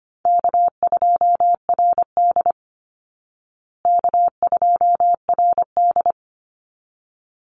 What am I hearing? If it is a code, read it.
Morse code, heard as a keyed tone, about 25 wpm, 700 Hz. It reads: K2RD K2RD
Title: Easy Call Signs Repeated 2x - 25wpm